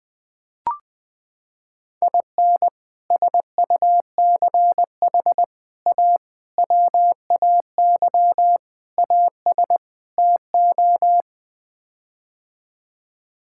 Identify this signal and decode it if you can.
Morse code, heard as a keyed tone, about 20 wpm, 700 Hz. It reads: IN SUCH A WAY AS TO
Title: Sets of 6 Words - 20wpm